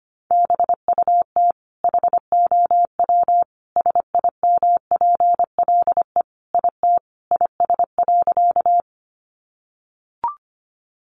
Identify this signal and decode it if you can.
Morse code, heard as a keyed tone, about 25 wpm, 700 Hz. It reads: BUT HOW SIMPLE IT IS.
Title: Sentences from Top 400 Words 25wpm